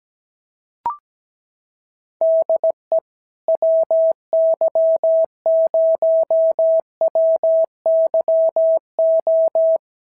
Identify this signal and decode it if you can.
Morse code, heard as a keyed tone, about 17 wpm, 650 Hz. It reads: DE WY0WYO
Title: SOTA QSO Practice 17wpm